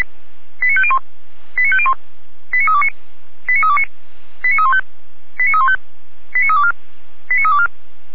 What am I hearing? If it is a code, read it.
Five-tone selective call (ZVEI), heard as quick runs of five tones, tones of about 70 ms. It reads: 89501 89501 89329 89329 89315 89315 89324 89324